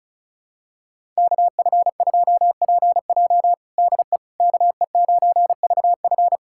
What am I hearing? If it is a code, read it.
Morse code, heard as a keyed tone, about 34 wpm, 700 Hz. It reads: KF2PJ DE KE9VF
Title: USA Callsign DE Callsign - 5 Characters - 35wpm